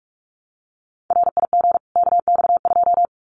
Morse code, heard as a keyed tone, about 44 wpm, 700 Hz. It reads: RIG KX2
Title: Common QSO Phrases 45wpm